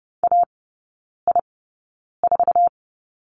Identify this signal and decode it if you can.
Morse code, heard as a keyed tone, about 30 wpm, spaced as for 9 wpm, 700 Hz. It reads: AI4
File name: 081 Lesson 14e - Number 2 - Three Characters 30wpm